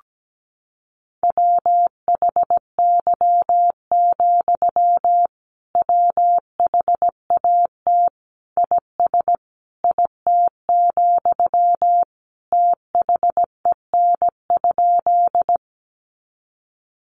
Morse code, heard as a keyed tone, about 17 wpm, 700 Hz. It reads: WHY, WHAT IS IT, THEN?
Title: Sentences from Top 200 Words 17wpm